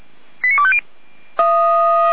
Five-tone selective call (ZVEI), heard as quick runs of five tones, tones of about 70 ms. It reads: 89250
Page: /455/SDR/ZVEI/89570/